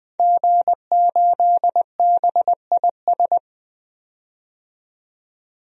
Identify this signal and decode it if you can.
Morse code, heard as a keyed tone, about 20 wpm, 700 Hz. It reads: G8BIS
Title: DXCC-100 Call Signs - 5 Characters - 20wpm